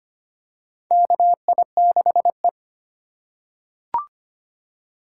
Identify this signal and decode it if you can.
Morse code, heard as a keyed tone, about 25 wpm, 700 Hz. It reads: KI6E
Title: USA Call Signs - 4 Characters - 2x Character Spacing - 25wpm